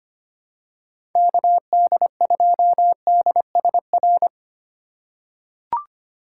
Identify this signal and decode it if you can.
Morse code, heard as a keyed tone, about 25 wpm, 700 Hz. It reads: KD2DSR
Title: USA Call Signs - 6 Characters - 4x Character Spacing - 25wpm